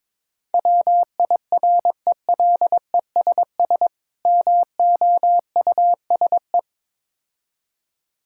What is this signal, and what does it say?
Morse code, heard as a keyed tone, about 23 wpm, 690 Hz. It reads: WIRELESS MOUSE
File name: Difficult Words with Hints - 22wpm